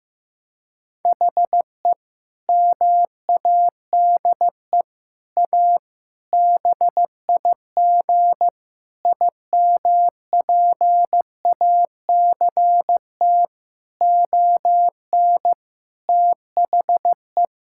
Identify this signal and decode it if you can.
Morse code, heard as a keyed tone, about 15 wpm, 700 Hz. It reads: HE MADE A BIG IMPACT ON THE
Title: Sentences from Top 1000 Words - Review Part 6 - 15wpm